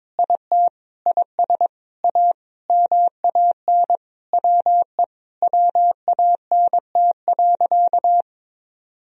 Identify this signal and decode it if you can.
Morse code, heard as a keyed tone, about 22 wpm, 700 Hz. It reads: IT IS A MAN WE WANT.
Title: Sentences from Top 200 Words and Abbreviations 22wpm